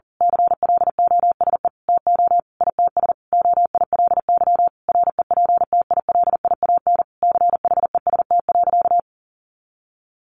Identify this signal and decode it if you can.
Morse code, heard as a keyed tone, about 37 wpm, 700 Hz. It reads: CLOSE TO ITS OILY REPTILIAN CHEST.
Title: A Sound Of Thunder 40wpm